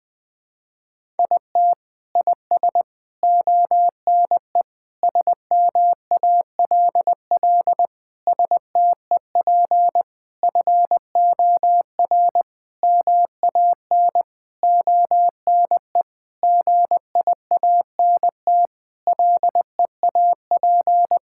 Morse code, heard as a keyed tone, about 20 wpm, 700 Hz. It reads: IT IS ONE SMALL STEP FOR MAN ONE GIANT LEAP
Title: Sentences with 9-Word Phrases - Spoken Beginning - 20wpm